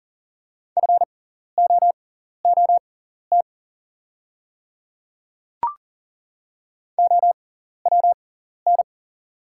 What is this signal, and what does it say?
Morse code, heard as a keyed tone, about 40 wpm, spaced as for 14 wpm, 700 Hz. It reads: FOOT OWN
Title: 6x Character Spacing - Top 500 Words - 40wpm